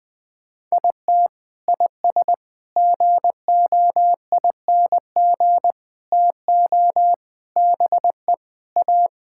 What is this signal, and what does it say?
Morse code, heard as a keyed tone, about 20 wpm, 700 Hz. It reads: IT IS GOING TO BE A
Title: Sentences with 4-Word Phrases - Spoken Beginning - 20wpm